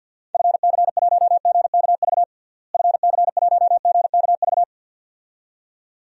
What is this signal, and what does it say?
Morse code, heard as a keyed tone, about 47 wpm, 700 Hz. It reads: FC1GKV FC1GKV
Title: DXCC-100 Call Signs - 6 Characters - Repeated 2x - 50wpm